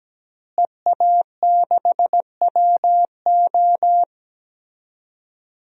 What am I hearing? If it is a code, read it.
Morse code, heard as a keyed tone, about 17 wpm, 700 Hz. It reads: EA6WO
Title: DXCC-100 Call Signs - 5 Characters - Repeated 3x - 17wpm